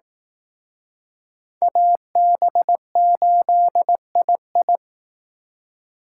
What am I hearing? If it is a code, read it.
Morse code, heard as a keyed tone, about 19 wpm, 700 Hz. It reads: AB8II
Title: Speed Racing - USA 2x2 Call Signs 15wpm